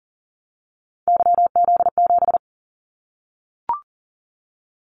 Morse code, heard as a keyed tone, about 40 wpm, 700 Hz. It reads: YZ7
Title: Callsign Prefixes for Top 25 DXCC - 40wpm